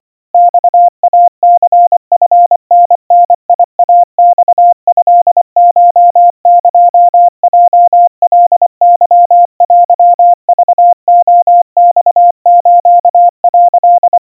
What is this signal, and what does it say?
Morse code (Japanese Wabun code), heard as a keyed tone, about 24 wpm, 700 Hz. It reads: マイニチタダイマトコエヲカケテクレマス」
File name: 90 cpm 700 Hz Crispy